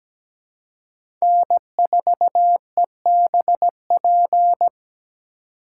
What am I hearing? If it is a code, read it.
Morse code, heard as a keyed tone, about 17 wpm, 700 Hz. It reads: N4EBP
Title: USA Call Signs - 5 Characters - 2x Character Spacing - 17wpm